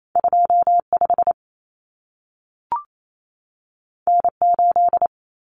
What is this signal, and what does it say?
Morse code, heard as a keyed tone, about 28 wpm, 700 Hz. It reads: 25 N8
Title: Common Cut Numbers - 2-Digit 28wpm